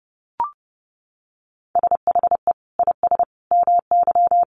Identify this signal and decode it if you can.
Morse code, heard as a keyed tone, about 30 wpm, 700 Hz. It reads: SHE IS MY
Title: Sentences from Top 300 Words 30wpm